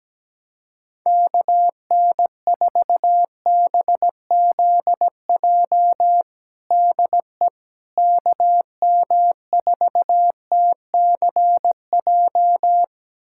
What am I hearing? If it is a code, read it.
Morse code, heard as a keyed tone, about 17 wpm, 700 Hz. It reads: KN4BZJ DE KM4TCJ
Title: USA Callsign DE Callsign - 6 Characters - 17wpm